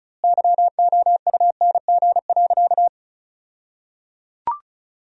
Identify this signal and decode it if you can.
Morse code, heard as a keyed tone, about 36 wpm, 700 Hz. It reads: YOUNG.
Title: 2x Word-Spacing - Sentences from Top 400 Words - 35wpm